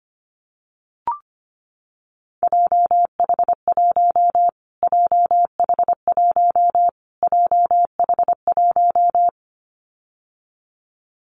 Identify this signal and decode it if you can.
Morse code, heard as a keyed tone, about 25 wpm, 700 Hz. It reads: JH1 JH1 JH1
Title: Callsign Prefixes for Top 25 DXCC - Repeated 3x - 25wpm